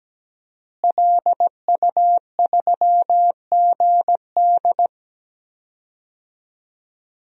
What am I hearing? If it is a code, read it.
Morse code, heard as a keyed tone, about 17 wpm, 700 Hz. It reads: LU3GD
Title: DXCC-100 Call Signs - 5 Characters - Repeated 2x - 17wpm